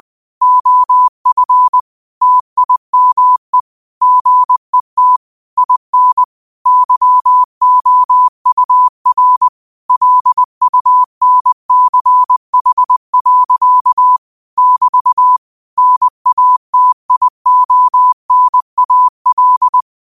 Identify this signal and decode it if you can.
Morse code, heard as a keyed tone, about 20 wpm, 1000 Hz. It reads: OF TIME GET IN YOUR LUNCH. <BT> NATIONAL